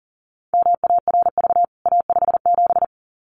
Morse code, heard as a keyed tone, about 40 wpm, 700 Hz. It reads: MARV A57
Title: CWT Exchange 40wpm